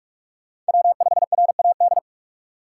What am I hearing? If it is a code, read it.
Morse code, heard as a keyed tone, about 56 wpm, 700 Hz. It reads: W5RAD